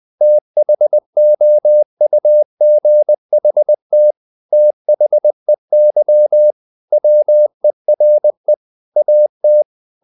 Morse code, heard as a keyed tone, about 20 wpm, 600 Hz. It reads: THOUGHT THEY WERE AT
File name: the old man and the sea / the old man and the sea part 1 20wpm